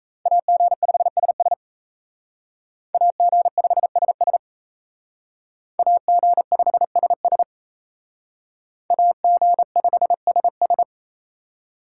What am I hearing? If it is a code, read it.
Morse code, heard as a keyed tone, about 35 wpm, 700 Hz. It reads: AG5SS AG5SS AG5SS AG5SS